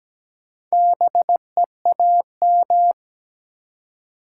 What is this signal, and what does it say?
Morse code, heard as a keyed tone, about 17 wpm, 700 Hz. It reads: BEAM
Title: Top 90 QSO Elements 17wpm